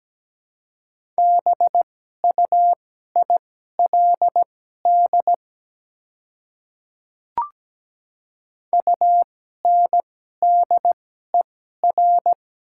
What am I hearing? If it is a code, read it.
Morse code, heard as a keyed tone, about 17 wpm, spaced as for 12 wpm, 700 Hz. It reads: BUILD UNDER
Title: 2x Character Spacing - Top 200 Words - 17wpm